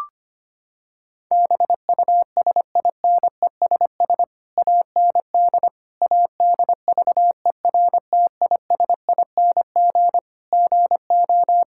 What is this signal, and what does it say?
Morse code, heard as a keyed tone, about 25 wpm, 700 Hz. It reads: BUSINESS AND ADVERTISING GO
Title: Sentences from Top 1000 Words - Review Part 6 - 25wpm